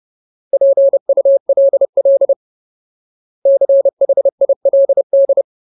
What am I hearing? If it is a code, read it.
Morse code, heard as a keyed tone, about 30 wpm, 550 Hz. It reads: PULL CHILD
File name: Ultimate Word List / 30WPM Ultimate